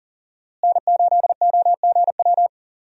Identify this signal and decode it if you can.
Morse code, heard as a keyed tone, about 37 wpm, 700 Hz. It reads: N8OGW
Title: USA Call Signs - 5 Characters - Repeated 2x - 40wpm